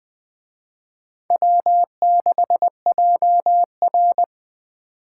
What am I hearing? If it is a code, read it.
Morse code, heard as a keyed tone, about 20 wpm, 700 Hz. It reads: W6JR
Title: Speed Racing - USA 1x2 Call Signs 17wpm